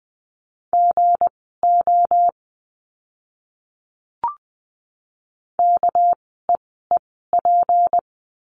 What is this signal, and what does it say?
Morse code, heard as a keyed tone, about 20 wpm, spaced as for 14 wpm, 700 Hz. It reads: GO KEEP
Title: 237 Lesson 38f - Letter q - Words 20wpm